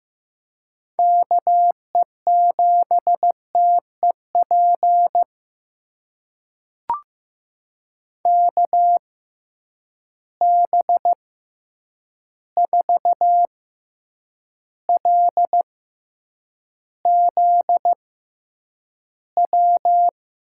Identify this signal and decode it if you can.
Morse code, heard as a keyed tone, about 15 wpm, spaced as for 8 wpm, 700 Hz. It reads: KE7TEP KB4LZW
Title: USA Call Signs - 6 Characters - 6x Character Spacing - 15wpm